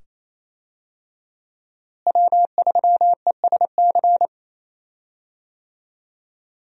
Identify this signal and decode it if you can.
Morse code, heard as a keyed tone, about 27 wpm, 700 Hz. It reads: W3ESC
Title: USA Call Signs - 5 Characters - Repeated 2x - 28wpm